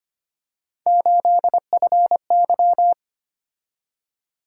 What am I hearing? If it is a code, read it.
Morse code, heard as a keyed tone, about 25 wpm, 700 Hz. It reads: 8FY
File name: Three Letter-Number Combinations - 25wpm